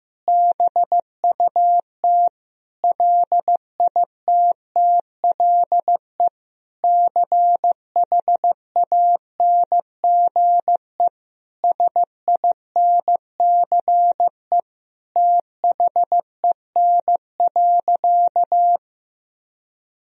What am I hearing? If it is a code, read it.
Morse code, heard as a keyed tone, about 15 wpm, 700 Hz. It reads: BUT LITTLE CHANGE SINCE THEN.
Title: Sentences from Top 1000 Words - Review Part 1 - 15wpm